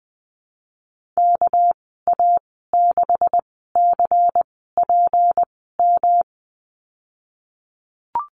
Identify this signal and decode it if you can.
Morse code, heard as a keyed tone, about 20 wpm, spaced as for 15 wpm, 700 Hz. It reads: KA6CPM
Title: 207 Lesson 33g - Letter j - Call Signs 20wpm